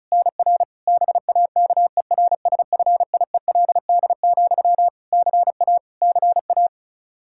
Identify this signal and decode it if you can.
Morse code, heard as a keyed tone, about 36 wpm, 700 Hz. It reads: NR BAKERSFIELD, CA CA
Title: Practice QSO - Exchange 1 - Beginning - 35wpm